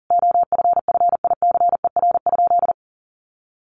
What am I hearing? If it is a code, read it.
Morse code, heard as a keyed tone, about 40 wpm, 700 Hz. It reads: OFFICER?
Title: Sentences from Top 1200 Words 40wpm